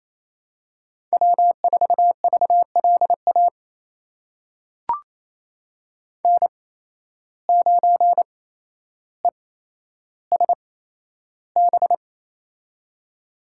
Morse code, heard as a keyed tone, about 27 wpm, spaced as for 12 wpm, 700 Hz. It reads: W4VLA N9ESB
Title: USA Call Signs - 5 Characters - 8x Character Spacing - 28wpm